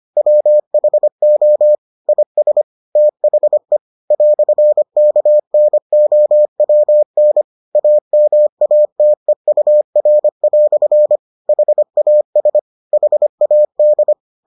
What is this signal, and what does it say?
Morse code, heard as a keyed tone, about 25 wpm, 600 Hz. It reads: WHO IS THE "KNOWN AMATEUR" HAS HAD